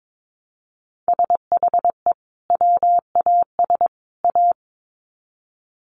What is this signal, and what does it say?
Morse code, heard as a keyed tone, about 22 wpm, 700 Hz. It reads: SHE WAS A
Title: Sets of 3 Words 22wpm